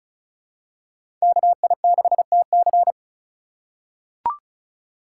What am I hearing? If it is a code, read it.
Morse code, heard as a keyed tone, about 36 wpm, 700 Hz. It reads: KI6TC
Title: USA Call Signs - 5 Characters - 3x Character Spacing - 35wpm